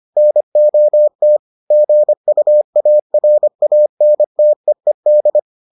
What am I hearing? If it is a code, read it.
Morse code, heard as a keyed tone, about 25 wpm, 600 Hz. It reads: NOT GUARANTEED